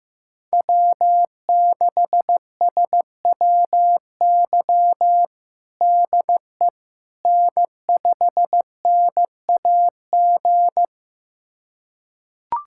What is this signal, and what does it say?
Morse code, heard as a keyed tone, about 15 wpm, 700 Hz. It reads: W6SWY DE N5NAG
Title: USA Callsign DE Callsign 15wpm